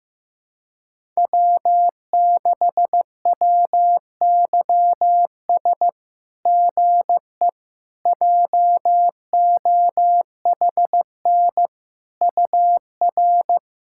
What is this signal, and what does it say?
Morse code, heard as a keyed tone, about 15 wpm, 700 Hz. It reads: W6WYS GE JOHN UR
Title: SOTA QSO Practice 15wpm